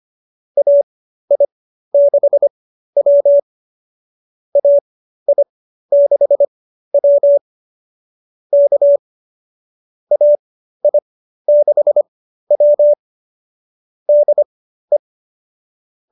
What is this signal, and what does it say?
Morse code, heard as a keyed tone, about 25 wpm, spaced as for 13 wpm, 590 Hz. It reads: AI6W AI6W K AI6W DE